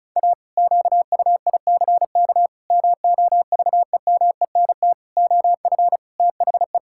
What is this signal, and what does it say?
Morse code, heard as a keyed tone, about 36 wpm, 700 Hz. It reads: A QUICK MOVEMENT OF THE
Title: Pangram Challenge v2 - 35wpm